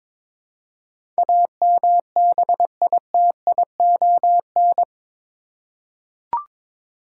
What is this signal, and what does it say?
Morse code, heard as a keyed tone, about 22 wpm, 700 Hz. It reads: AMBITION
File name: practice / 22wpm / Eight Letter Words - 22wpm